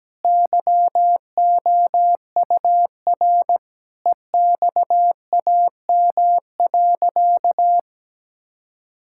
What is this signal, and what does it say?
Morse code, heard as a keyed tone, about 17 wpm, 700 Hz. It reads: YOUR EXAM.
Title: Sentences from Top 1000 Words - Review Part 5 - 17wpm